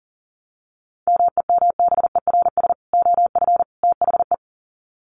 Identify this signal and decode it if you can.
Morse code, heard as a keyed tone, about 40 wpm, 700 Hz. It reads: MEMBERS OF THE
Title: Sets of 3 Words 40wpm